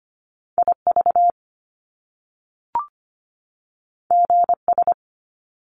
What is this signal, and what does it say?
Morse code, heard as a keyed tone, about 25 wpm, 700 Hz. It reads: IV GS
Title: Top Two Letter Combinations in Words 25wpm